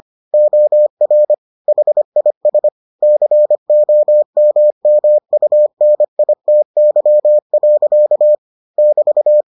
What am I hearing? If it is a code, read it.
Morse code, heard as a keyed tone, about 25 wpm, 600 Hz. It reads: OR HIS COMMUNITY. <BT>